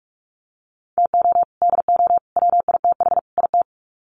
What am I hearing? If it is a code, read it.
Morse code, heard as a keyed tone, about 44 wpm, 700 Hz. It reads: TO DO WITH IT
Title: Sets of 4 Words 45wpm